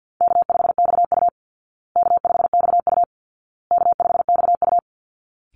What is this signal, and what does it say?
Morse code, heard as a keyed tone, about 50 wpm, 700 Hz. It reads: K5XU K5XU K5XU
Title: Easy Call Signs Repeated 3x - 50wpm